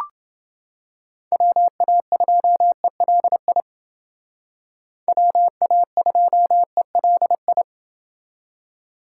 Morse code, heard as a keyed tone, about 28 wpm, 700 Hz. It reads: WA2ELI WA2ELI